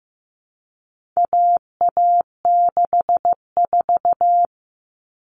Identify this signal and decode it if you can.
Morse code, heard as a keyed tone, about 15 wpm, 700 Hz. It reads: AA64
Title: All Cut Numbers - 4-Digit 15wpm